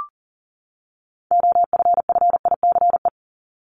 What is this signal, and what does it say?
Morse code, heard as a keyed tone, about 40 wpm, 700 Hz. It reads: OFFICE
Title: Top 751-1000 Words 40wpm